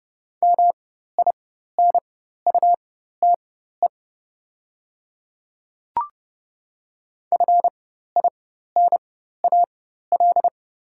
Morse code, heard as a keyed tone, about 30 wpm, spaced as for 14 wpm, 690 Hz. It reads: MINUTE FINAL
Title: 4x Character Spacing - Top 500 Words - 30wpm